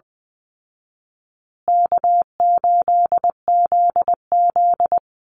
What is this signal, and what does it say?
Morse code, heard as a keyed tone, about 20 wpm, 700 Hz. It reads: K8ZZ
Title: Easy Call Signs Repeated 3x - 20wpm